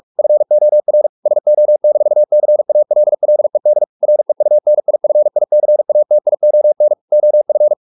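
Morse code, heard as a keyed tone, about 44 wpm, 600 Hz. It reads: FOR SO-CALLED REUNIFICATION OF
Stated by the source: Morse practice at 45 wpm, the novel Never by Ken Follett in English, from CW med Gnister.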